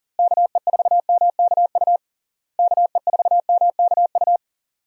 Morse code, heard as a keyed tone, about 40 wpm, 700 Hz. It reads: KE4MKU KE4MKU
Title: USA Call Signs - 6 Characters - Repeated 3x - 40wpm